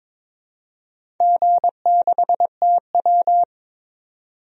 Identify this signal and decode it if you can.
Morse code, heard as a keyed tone, about 22 wpm, 700 Hz. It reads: G6TW
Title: DXCC-100 Call Signs - 4 Characters - 22wpm